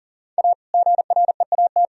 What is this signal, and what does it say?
Morse code, heard as a keyed tone, about 37 wpm, 700 Hz. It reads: A GREAT
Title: Sentences from Top 1000 Words - Review Part 2 - 40wpm